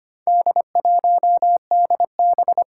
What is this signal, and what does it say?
Morse code, heard as a keyed tone, about 25 wpm, 700 Hz. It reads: D1DB
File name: practice / 25wpm / Four Letter-Number Combinations - 25wpm